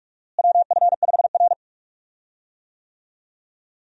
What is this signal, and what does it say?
Morse code, heard as a keyed tone, about 46 wpm, 700 Hz. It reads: WF5R
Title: USA Call Signs - 4 Characters - Repeated 3x - 45wpm